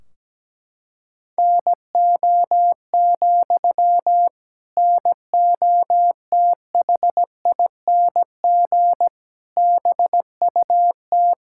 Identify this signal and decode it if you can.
Morse code, heard as a keyed tone, about 17 wpm, 700 Hz. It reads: NO, NOTHING BUT
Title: Sentences from Top 1600 Words 17wpm